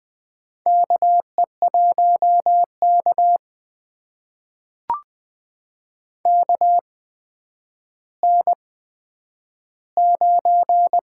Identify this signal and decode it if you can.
Morse code, heard as a keyed tone, about 20 wpm, spaced as for 10 wpm, 700 Hz. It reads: KE1K KN9
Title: USA Call Signs - 4 Characters - 8x Character Spacing - 20wpm